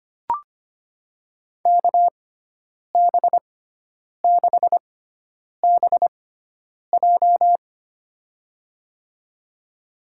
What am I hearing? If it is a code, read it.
Morse code, heard as a keyed tone, about 25 wpm, spaced as for 9 wpm, 700 Hz. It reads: KB6BJ
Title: USA Call Signs - 5 Characters - 6x Character Spacing - 25wpm